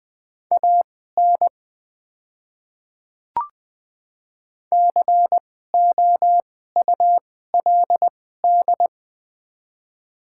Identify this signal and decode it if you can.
Morse code, heard as a keyed tone, about 20 wpm, spaced as for 14 wpm, 700 Hz. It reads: AN COULD
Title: 2x Character Spacing - Top 300 Words - 20wpm